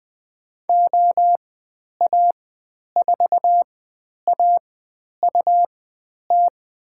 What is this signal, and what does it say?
Morse code, heard as a keyed tone, about 20 wpm, spaced as for 10 wpm, 700 Hz. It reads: OA4AUT
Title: 109 Lesson 18g - Letter m - Call Signs 20wpm